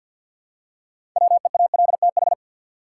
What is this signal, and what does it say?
Morse code, heard as a keyed tone, about 53 wpm, 700 Hz. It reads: WEALTH
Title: Six Letter Words - 50wpm